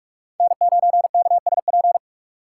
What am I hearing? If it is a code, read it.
Morse code, heard as a keyed tone, about 46 wpm, 700 Hz. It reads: N9KSP